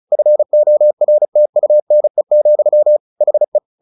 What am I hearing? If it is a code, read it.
Morse code, heard as a keyed tone, about 34 wpm, 600 Hz. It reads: FORTUNE, HE
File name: the old man and the sea 2 35wpm